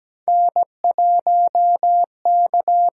Morse code, heard as a keyed tone, about 17 wpm, 700 Hz. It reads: N1K
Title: USA Call Signs - 8x Character Spacing - 17wpm